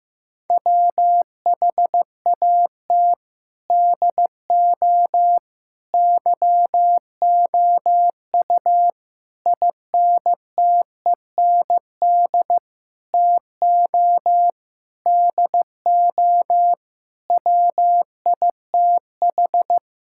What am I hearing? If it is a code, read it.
Morse code, heard as a keyed tone, about 15 wpm, 700 Hz. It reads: WHAT DO YOU INTEND TO DO WITH
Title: Sentences from Top 1600 Words 15wpm